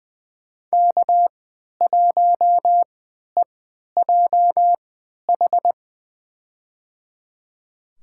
Morse code, heard as a keyed tone, about 20 wpm, spaced as for 11 wpm, 700 Hz. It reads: K1EJH
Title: USA Call Signs - 3x Character Spacing - 20wpm